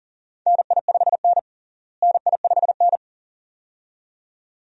Morse code, heard as a keyed tone, about 40 wpm, 700 Hz. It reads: NI5N NI5N
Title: USA Call Signs - 4 Characters - Repeated 2x - 40wpm